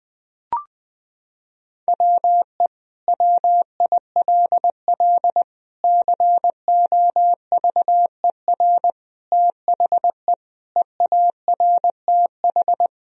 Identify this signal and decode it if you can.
Morse code, heard as a keyed tone, about 20 wpm, 700 Hz. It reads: WE WILL COVER THE EARTH
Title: Sentences from Top 1000 Words - Review Part 1 - 20wpm